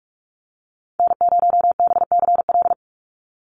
Morse code, heard as a keyed tone, about 44 wpm, 700 Hz. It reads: N0BCL
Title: USA Call Signs 45wpm